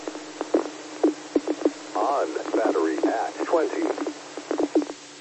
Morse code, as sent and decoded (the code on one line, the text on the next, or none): none